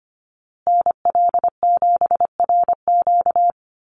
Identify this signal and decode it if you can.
Morse code, heard as a keyed tone, about 25 wpm, 700 Hz. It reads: NL7RQ
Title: USA Call Signs Repeated 3x 25wpm